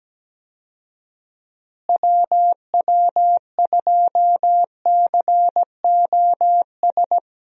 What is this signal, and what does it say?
Morse code, heard as a keyed tone, about 17 wpm, 700 Hz. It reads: WW2COS
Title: USA Call Signs - 6 Characters - Repeated 3x - 17wpm